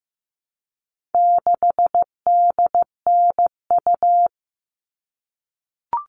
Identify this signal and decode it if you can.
Morse code, heard as a keyed tone, about 15 wpm, 700 Hz. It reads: 6DNU
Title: All Cut Numbers - 4-Digit 15wpm